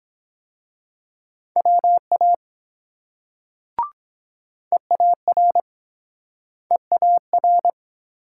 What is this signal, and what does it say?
Morse code, heard as a keyed tone, about 23 wpm, 700 Hz. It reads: WA EAR EAR